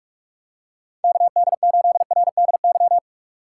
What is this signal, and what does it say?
Morse code, heard as a keyed tone, about 47 wpm, 700 Hz. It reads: KD8RDY
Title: USA Call Signs - 6 Characters - 3x Character Spacing - 45wpm